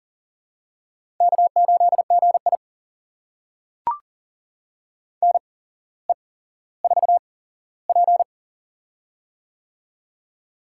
Morse code, heard as a keyed tone, about 40 wpm, spaced as for 17 wpm, 700 Hz. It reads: K8GI NE4P
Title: USA Call Signs - 4 Characters - 8x Character Spacing - 40wpm